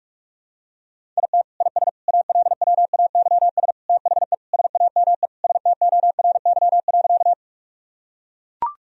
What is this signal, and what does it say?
Morse code, heard as a keyed tone, about 42 wpm, 700 Hz. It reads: IT IS ALWAYS THE SAME STORY.
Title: Sentences from Top 1000 Words - Review Part 1 - 45wpm